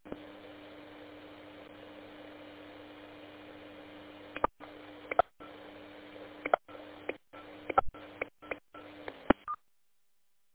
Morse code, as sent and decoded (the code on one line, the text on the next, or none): none